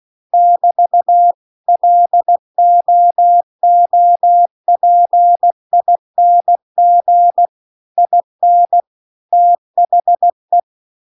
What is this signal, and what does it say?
Morse code, heard as a keyed tone, about 16 wpm, 700 Hz. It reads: <BT> LOOPING IN THE